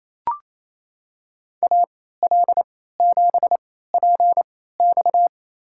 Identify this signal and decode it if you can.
Morse code, heard as a keyed tone, about 28 wpm, spaced as for 16 wpm, 700 Hz. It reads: AL7PX
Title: USA Call Signs - 3x Character Spacing - 28wpm